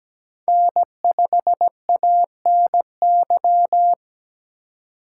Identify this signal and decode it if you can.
Morse code, heard as a keyed tone, about 17 wpm, 700 Hz. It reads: N5ANY
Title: USA Call Signs - 5 Characters - 4x Character Spacing - 17wpm